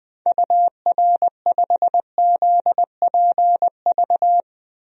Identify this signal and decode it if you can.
Morse code, heard as a keyed tone, about 21 wpm, 700 Hz. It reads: UR5ZPV
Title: DXCC-100 Call Signs - 6 Characters - 20wpm